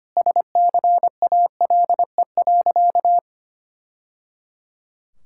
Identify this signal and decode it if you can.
Morse code, heard as a keyed tone, about 25 wpm, 700 Hz. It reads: SCALE.
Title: Sentences from Top 1000 Words - Review Part 5 - 25wpm